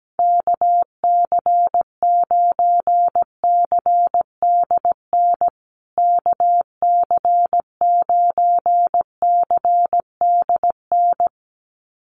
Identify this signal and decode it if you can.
Morse code, heard as a keyed tone, about 17 wpm, 700 Hz. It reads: KC9CDN KC9CDN
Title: USA Call Signs Repeated 2x 17wpm